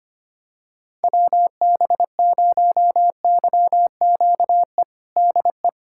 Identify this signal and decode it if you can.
Morse code, heard as a keyed tone, about 25 wpm, 700 Hz. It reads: WB0YQE DE
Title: USA Callsign DE Callsign - 6 Characters - 25wpm